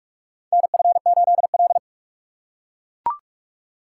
Morse code, heard as a keyed tone, about 42 wpm, 700 Hz. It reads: NF8L
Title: USA Call Signs - 4 Characters - 3x Character Spacing - 45wpm